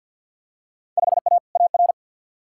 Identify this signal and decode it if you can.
Morse code, heard as a keyed tone, about 50 wpm, 700 Hz. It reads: 5A AR
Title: Field Day Exchanges Encore - 50wpm